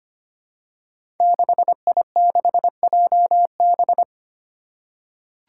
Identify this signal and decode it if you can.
Morse code, heard as a keyed tone, about 25 wpm, 700 Hz. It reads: 6I6JB
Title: Five Challenging Letter-Number Combinations 25wpm